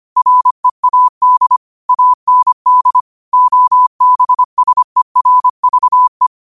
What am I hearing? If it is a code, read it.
Morse code, heard as a keyed tone, about 25 wpm, 1000 Hz. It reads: READ AND OBSERVE
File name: qotdfiles / 25wpm